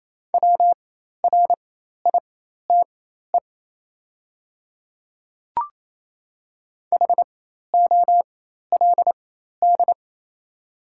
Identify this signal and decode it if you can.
Morse code, heard as a keyed tone, about 28 wpm, spaced as for 13 wpm, 700 Hz. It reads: WRITE HOLD
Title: 4x Character Spacing - Top 500 Words - 28wpm